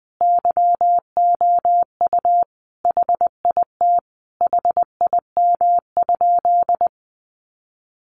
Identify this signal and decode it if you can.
Morse code, heard as a keyed tone, about 19 wpm, 700 Hz. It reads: YOU HIT HIM?
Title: Sentences from Top 1200 Words 20wpm